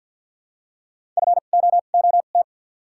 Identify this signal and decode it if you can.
Morse code, heard as a keyed tone, about 53 wpm, 700 Hz. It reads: F O O T
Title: 2x Character Spacing - Top 500 Words - 50wpm